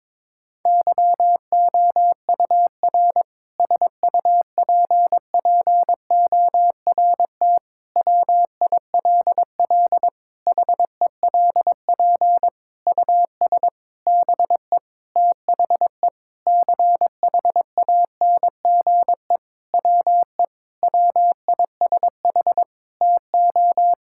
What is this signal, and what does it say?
Morse code, heard as a keyed tone, about 23 wpm, 700 Hz. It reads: YOUR SUPPORT WILL HELP US BE THE CHANGE WE WISH TO
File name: Sentences with 7-Word Phrases - 22wpm